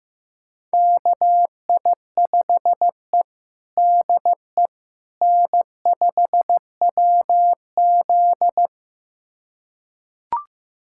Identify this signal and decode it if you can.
Morse code, heard as a keyed tone, about 15 wpm, 700 Hz. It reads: KI5E DE N5WZ
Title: USA Callsign DE Callsign - 4 Characters - 15wpm